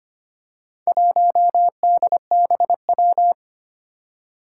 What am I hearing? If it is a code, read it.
Morse code, heard as a keyed tone, about 25 wpm, 700 Hz. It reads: 1DBW
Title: Four Challenging Letter-Number Combinations 25wpm